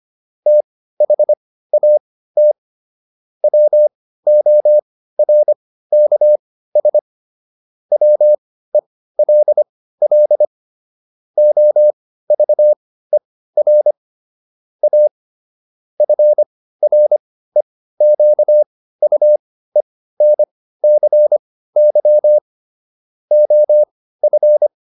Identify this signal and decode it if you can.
Morse code, heard as a keyed tone, about 25 wpm, spaced as for 15 wpm, 600 Hz. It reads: THAT WORKS WELL OVER A FREQUENCY OF